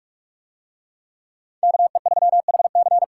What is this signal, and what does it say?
Morse code, heard as a keyed tone, about 49 wpm, 700 Hz. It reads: KE3HC